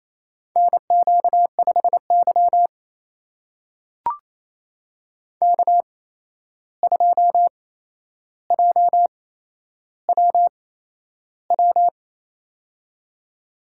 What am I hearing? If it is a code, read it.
Morse code, heard as a keyed tone, about 28 wpm, spaced as for 11 wpm, 700 Hz. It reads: NQ5Y K2JWW
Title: USA Call Signs - 8x Character Spacing - 28wpm